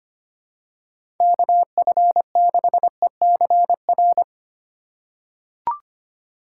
Morse code, heard as a keyed tone, about 25 wpm, 700 Hz. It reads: KF6ECR
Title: USA Call Signs - 6 Characters - 2x Character Spacing - 25wpm